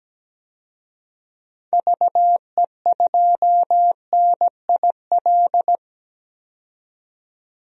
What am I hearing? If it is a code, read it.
Morse code, heard as a keyed tone, about 17 wpm, 700 Hz. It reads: VE2NIL
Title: DXCC-100 Call Signs - 6 Characters - 17wpm